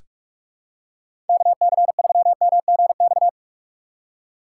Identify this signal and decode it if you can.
Morse code, heard as a keyed tone, about 46 wpm, 700 Hz. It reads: KC3MGX
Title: USA Call Signs - 4x Character Spacing - 45wpm